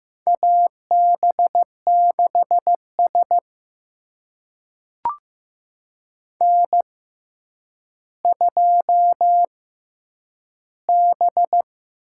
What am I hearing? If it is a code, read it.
Morse code, heard as a keyed tone, about 15 wpm, spaced as for 9 wpm, 700 Hz. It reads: AB6S N2B
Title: USA Call Signs - 4 Characters - 6x Character Spacing - 15wpm